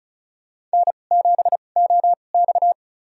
Morse code, heard as a keyed tone, about 34 wpm, spaced as for 25 wpm, 700 Hz. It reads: N7OX
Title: USA Call Signs - 4 Characters - 2x Character Spacing - 35wpm